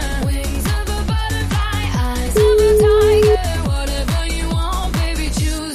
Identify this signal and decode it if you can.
Ringback tone (Europe, 425 Hz), with noise.
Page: Uvítací melodii uslyší volající, než přijmete hovor.